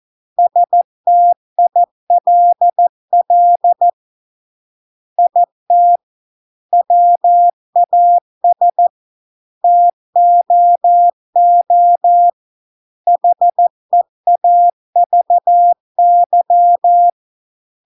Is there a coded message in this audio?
Yes: Morse code from 0.4 to 17.1 s, heard as a keyed tone, about 14 wpm, 700 Hz. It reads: STILL IT WAS TOO HEAVY